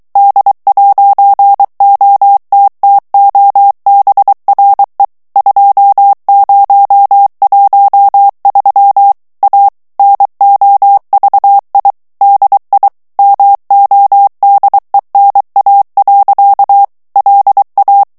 Morse code, heard as a keyed tone, about 23 wpm, 800 Hz. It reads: D'OTTOBRE 2013 A NOVI DI MODENA. LA
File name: ITALIEN texte clair 1 vitesse 22 mots minute